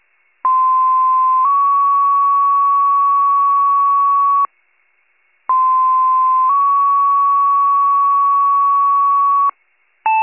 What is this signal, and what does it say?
Two-tone page, heard as a one-second tone, then a three-second tone, about 1040 then 1120 Hz, about 1040 then 1120 Hz. It is heard from 0.4 to 9.5 s.